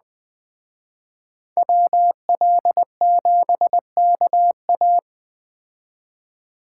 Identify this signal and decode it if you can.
Morse code, heard as a keyed tone, about 20 wpm, 700 Hz. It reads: WL7KA